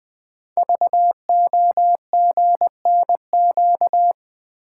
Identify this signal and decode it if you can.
Morse code, heard as a keyed tone, about 20 wpm, 700 Hz. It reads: VOGNQ
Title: Code Groups - 5 Letters - 20wpm